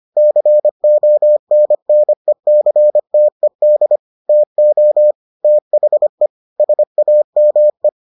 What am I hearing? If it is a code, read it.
Morse code, heard as a keyed tone, about 25 wpm, 600 Hz. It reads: CONNECTED TO THE SAME